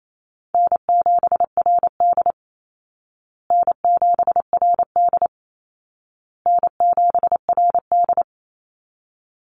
Morse code, heard as a keyed tone, about 28 wpm, 700 Hz. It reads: N7RD N7RD N7RD
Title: Easy Call Signs Repeated 3x - 28wpm